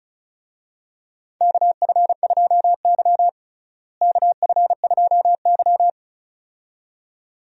Morse code, heard as a keyed tone, about 34 wpm, 700 Hz. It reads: KF2Y KF2Y
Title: USA Call Signs - 4 Characters - Repeated 2x - 35wpm